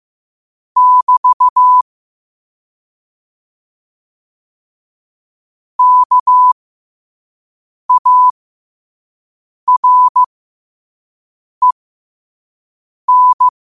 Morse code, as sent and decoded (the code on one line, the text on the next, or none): -...- / -.- .- .-. -.
<BT> KAREN